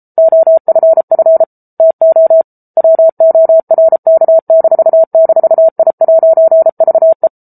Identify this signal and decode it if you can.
Morse code, heard as a keyed tone, about 34 wpm, 650 Hz. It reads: OFF TO WORK--I'VE
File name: Book Two / 07-Chapter 7 - 33 WPM